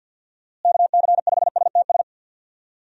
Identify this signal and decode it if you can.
Morse code, heard as a keyed tone, about 47 wpm, 700 Hz. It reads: KC5STS
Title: USA Call Signs - 6 Characters - Repeated 3x - 50wpm